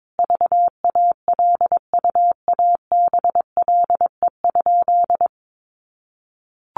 Morse code, heard as a keyed tone, about 22 wpm, 700 Hz. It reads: VALUABLE?
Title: Sentences from Top 1100 Words 22wpm